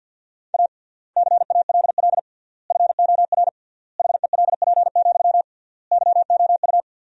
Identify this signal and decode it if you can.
Morse code, heard as a keyed tone, about 53 wpm, 700 Hz. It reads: A CALL FOR HELP, YOU
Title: 3x Word-Spacing - Sentences from Top 300 Words - 50wpm